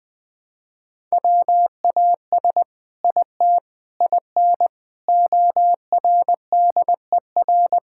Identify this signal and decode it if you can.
Morse code, heard as a keyed tone, about 20 wpm, 700 Hz. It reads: WAS IT IN ORDER
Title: Sentences from Top 1000 Words - Review Part 6 - 20wpm